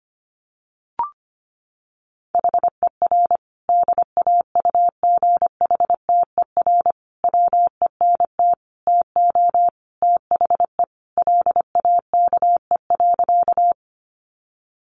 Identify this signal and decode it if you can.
Morse code, heard as a keyed tone, about 25 wpm, 700 Hz. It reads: HER DAUGHTER WENT TO THE LAKE.
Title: Sentences from Top 1500 Words 25wpm